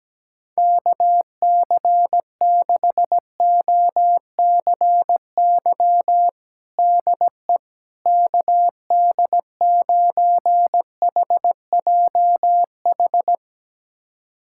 Morse code, heard as a keyed tone, about 17 wpm, 700 Hz. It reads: KC6OCY DE KD9HJH
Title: USA Callsign DE Callsign - 6 Characters - 17wpm